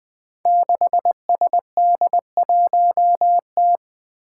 Morse code, heard as a keyed tone, about 20 wpm, 700 Hz. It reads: 6SD1T
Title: Five Challenging Letter-Number Combinations 20wpm